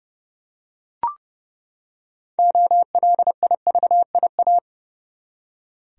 Morse code, heard as a keyed tone, about 30 wpm, 700 Hz. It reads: OLIVIA
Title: USA Common Names - 30wpm